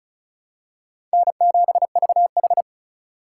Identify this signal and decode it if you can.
Morse code, heard as a keyed tone, about 34 wpm, 700 Hz. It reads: N7VH
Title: USA Call Signs - 4 Characters - 4x Character Spacing - 35wpm